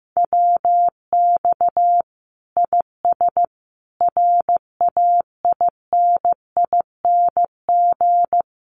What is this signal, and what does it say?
Morse code, heard as a keyed tone, about 15 wpm, 700 Hz. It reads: WX IS RAINING
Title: Common QSO Phrases 15wpm